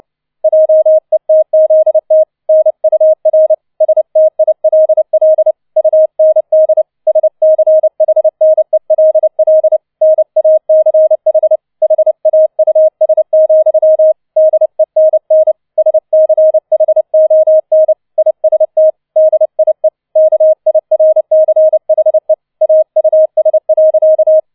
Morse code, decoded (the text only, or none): JETZT NUR STILL UND SCHNELL NACH HAUS, DENN SCHON IST DIE KIRCHE AUS.